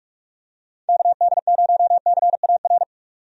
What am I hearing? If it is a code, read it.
Morse code, heard as a keyed tone, about 46 wpm, 700 Hz. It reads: KD0CAR
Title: USA Call Signs - 6 Characters - 45wpm